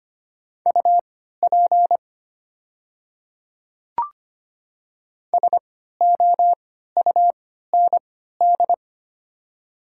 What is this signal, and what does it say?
Morse code, heard as a keyed tone, about 25 wpm, spaced as for 14 wpm, 700 Hz. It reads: UP SOUND
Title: 3x Character Spacing - Top 100 Words - 25wpm